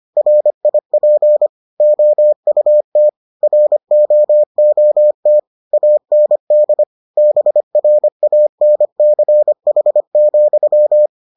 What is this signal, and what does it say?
Morse code, heard as a keyed tone, about 25 wpm, 600 Hz. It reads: RIP OUT ROOT AND BRANCH,